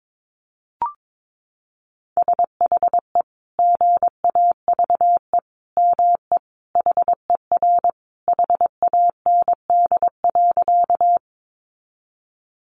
Morse code, heard as a keyed tone, about 22 wpm, 700 Hz. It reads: SHE GAVE ME HER HAND.
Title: Sentences from Top 500 Words 22wpm